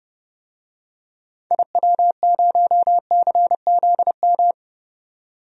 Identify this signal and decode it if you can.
Morse code, heard as a keyed tone, about 30 wpm, 700 Hz. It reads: IW0CZM
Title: DXCC-100 Call Signs - 6 Characters - 30wpm